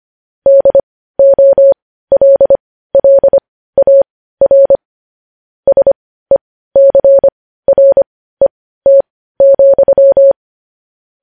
Morse code, heard as a keyed tone, about 25 wpm, spaced as for 15 wpm, 550 Hz. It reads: DOLLAR SECRET,
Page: news-headlines-in-morse-code-at-15-wpm